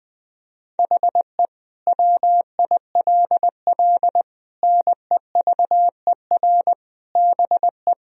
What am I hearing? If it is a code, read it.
Morse code, heard as a keyed tone, about 19 wpm, 700 Hz. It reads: HE WILL NEVER BE
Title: Sentences from Top 1000 Words - Review Part 2 - 20wpm